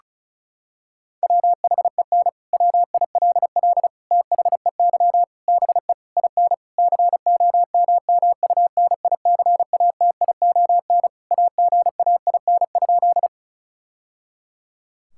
Morse code, heard as a keyed tone, about 36 wpm, 700 Hz. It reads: WHEN WILL THEY BE IN COMMUNICATION AGAIN?
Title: Sentences from Top 1000 Words - Review Part 6 - 35wpm